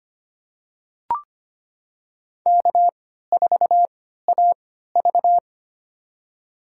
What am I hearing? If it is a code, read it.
Morse code, heard as a keyed tone, about 25 wpm, spaced as for 14 wpm, 700 Hz. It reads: K4AV
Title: USA Call Signs - 4 Characters - 3x Character Spacing - 25wpm